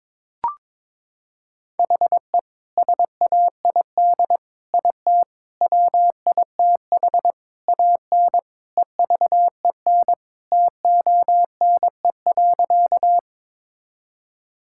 Morse code, heard as a keyed tone, about 22 wpm, 700 Hz. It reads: HE SAID IT WITH AN EVEN TONE.
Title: Sentences from Top 1600 Words 22wpm